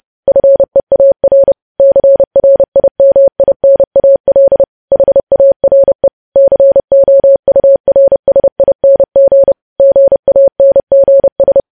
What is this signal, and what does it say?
Morse code, heard as a keyed tone, about 30 wpm, 550 Hz. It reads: FEAR CRIMINAL HARE COURSING GANGS
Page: news headlines in morse code at 30 wpm